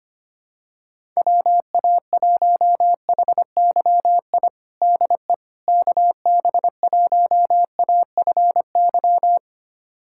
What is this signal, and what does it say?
Morse code, heard as a keyed tone, about 25 wpm, 700 Hz. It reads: WA1HYI DE KB1AFY
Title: USA Callsign DE Callsign - 6 Characters - 25wpm